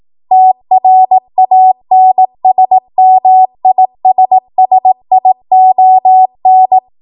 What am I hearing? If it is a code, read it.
Morse code, heard as a keyed tone, about 18 wpm, 750 Hz. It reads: TRANSMISSION
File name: Archive / 18 WPM / 18WPM